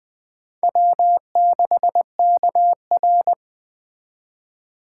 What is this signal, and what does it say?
Morse code, heard as a keyed tone, about 21 wpm, 700 Hz. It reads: W6KR